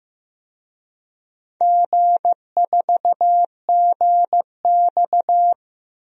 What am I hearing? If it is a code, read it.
Morse code, heard as a keyed tone, about 15 wpm, 700 Hz. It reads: G4GX
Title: DXCC-100 Call Signs - 4 Characters - 15wpm